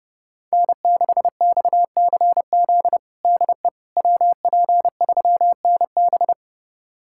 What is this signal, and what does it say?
Morse code, heard as a keyed tone, about 30 wpm, 700 Hz. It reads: N6XCZ DE WP3NB
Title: USA Callsign DE Callsign - 5 Characters - 30wpm